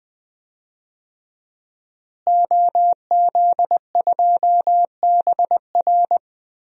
Morse code, heard as a keyed tone, about 20 wpm, 700 Hz. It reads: OZ2BR
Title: DXCC-100 Call Signs - 5 Characters - Repeated 3x - 20wpm